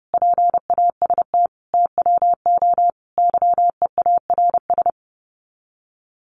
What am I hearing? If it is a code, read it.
Morse code, heard as a keyed tone, about 30 wpm, 700 Hz. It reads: PAST TWO YEARS
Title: Sets of 5 Words 30wpm